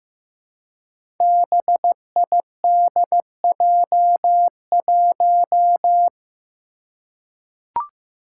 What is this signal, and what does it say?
Morse code, heard as a keyed tone, about 15 wpm, 700 Hz. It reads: BIDJ1
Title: Five Challenging Letter-Number Combinations 15wpm